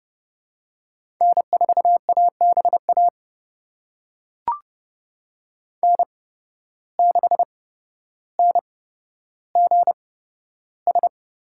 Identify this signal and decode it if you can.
Morse code, heard as a keyed tone, about 30 wpm, spaced as for 13 wpm, 700 Hz. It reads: N4ABA N6NGS
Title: USA Call Signs - 5 Characters - 8x Character Spacing - 30wpm